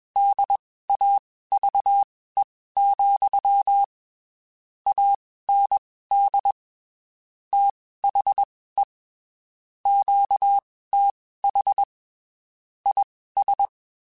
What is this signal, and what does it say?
Morse code, heard as a keyed tone, about 21 wpm, spaced as for 15 wpm, 800 Hz. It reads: DAVE, AND THE QTH IS
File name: (spaced) / QSOs Spaced-mp3s / kc7haa @20 sp